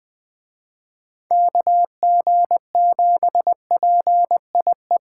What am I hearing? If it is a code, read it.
Morse code, heard as a keyed tone, about 20 wpm, 700 Hz. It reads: KG7PIE
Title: Speed Racing - USA 2x3 Call Signs 17wpm